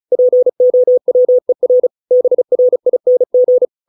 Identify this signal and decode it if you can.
Morse code, heard as a keyed tone, about 34 wpm, 500 Hz. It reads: POWER BRING